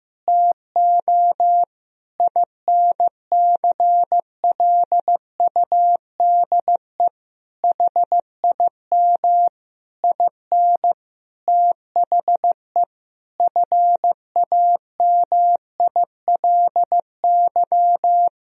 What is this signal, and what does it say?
Morse code, heard as a keyed tone, about 16 wpm, 700 Hz. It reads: TO INCLUDE HIM IN THE FAMILY
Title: Sentences from Top 1000 Words - Review Part 3 - 15wpm